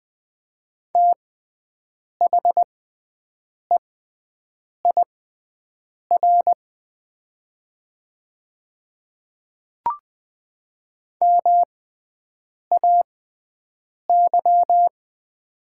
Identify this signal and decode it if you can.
Morse code, heard as a keyed tone, about 20 wpm, spaced as for 7 wpm, 700 Hz. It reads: THEIR MAY
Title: 6x Character Spacing - Top 100 Words - 20wpm